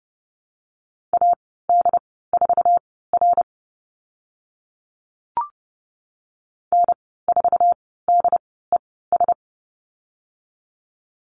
Morse code, heard as a keyed tone, about 30 wpm, spaced as for 17 wpm, 700 Hz. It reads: AD4R N4DES
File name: Lesson 14g - Number 2 - Call Signs 30wpm